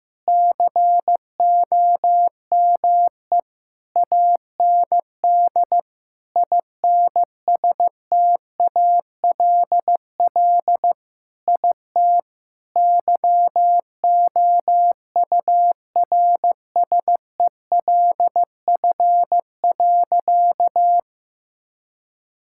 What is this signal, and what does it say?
Morse code, heard as a keyed tone, about 15 wpm, 700 Hz. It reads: COME AND INSTALL IT YOURSELF.
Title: Sentences from Top 1600 Words 15wpm